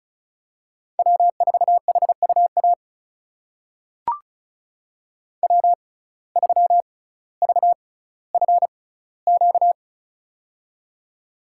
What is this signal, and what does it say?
Morse code, heard as a keyed tone, about 36 wpm, spaced as for 18 wpm, 700 Hz. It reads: W4HUA W3VFQ
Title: USA Call Signs - 6x Character Spacing - 35wpm